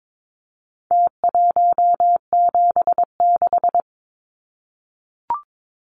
Morse code, heard as a keyed tone, about 22 wpm, 700 Hz. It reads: T176
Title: Numbers - Contest Style - 4-Digit 22wpm